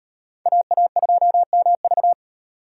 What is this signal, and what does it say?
Morse code, heard as a keyed tone, about 38 wpm, 700 Hz. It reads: AA2MV